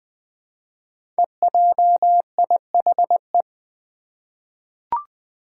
Morse code, heard as a keyed tone, about 20 wpm, 700 Hz. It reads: EJIHE
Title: Five Challenging Letter-Number Combinations 20wpm